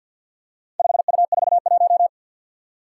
Morse code, heard as a keyed tone, about 52 wpm, 700 Hz. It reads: 5U41
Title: Four Challenging Letter-Number Combinations 50wpm